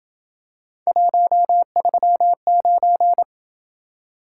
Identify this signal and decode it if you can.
Morse code, heard as a keyed tone, about 26 wpm, 700 Hz. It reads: 139